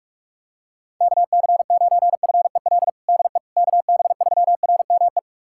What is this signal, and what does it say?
Morse code, heard as a keyed tone, about 46 wpm, 700 Hz. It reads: KC9FEL DE KB3RME
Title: USA Callsign DE Callsign - 6 Characters - 45wpm